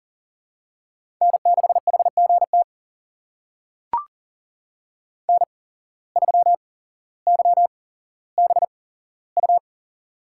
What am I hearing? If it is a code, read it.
Morse code, heard as a keyed tone, about 40 wpm, spaced as for 17 wpm, 700 Hz. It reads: N6HGT N3YBU
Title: USA Call Signs - 5 Characters - 8x Character Spacing - 40wpm